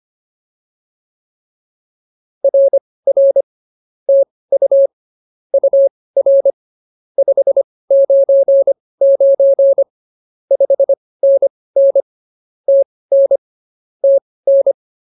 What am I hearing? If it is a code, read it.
Morse code, heard as a keyed tone, about 24 wpm, spaced as for 18 wpm, 550 Hz. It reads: RR TU UR 599 5NN TN TN
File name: POTA207 18